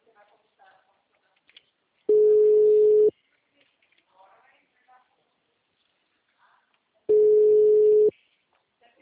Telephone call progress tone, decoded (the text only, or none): ringback tone